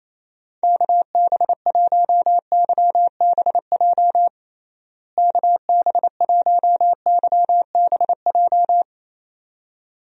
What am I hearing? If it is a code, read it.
Morse code, heard as a keyed tone, about 27 wpm, 700 Hz. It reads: KB1YBJ KB1YBJ
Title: USA Call Signs - 6 Characters - Repeated 2x - 28wpm